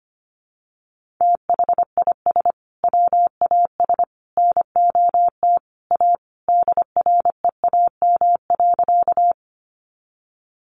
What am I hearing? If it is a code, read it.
Morse code, heard as a keyed tone, about 25 wpm, 700 Hz. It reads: THIS WAS NOT A DREAM.
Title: Sentences from Top 1500 Words 25wpm